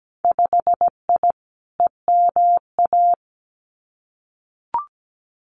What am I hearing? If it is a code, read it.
Morse code, heard as a keyed tone, about 17 wpm, 700 Hz. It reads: 5I EMA
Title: Winter Field Day Exchanges Encore 17wpm